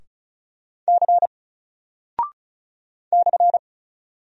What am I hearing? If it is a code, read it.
Morse code, heard as a keyed tone, about 35 wpm, 700 Hz. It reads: C /
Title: Single Character (Letter:Number:Punctuation:Prosign) 35wpm